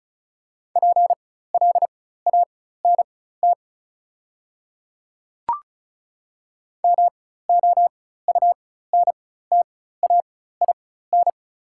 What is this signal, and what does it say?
Morse code, heard as a keyed tone, about 34 wpm, spaced as for 16 wpm, 690 Hz. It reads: PLANT MOUNTAIN
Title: 4x Character Spacing - Top 500 Words - 35wpm